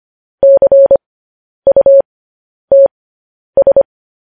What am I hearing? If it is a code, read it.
Morse code, heard as a keyed tone, about 25 wpm, spaced as for 10 wpm, 550 Hz. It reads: CUTS
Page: news headlines in morse code at 10 wpm